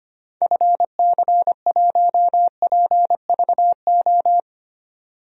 Morse code, heard as a keyed tone, about 25 wpm, 700 Hz. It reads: FC1PVO
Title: DXCC-100 Call Signs - 6 Characters - 25wpm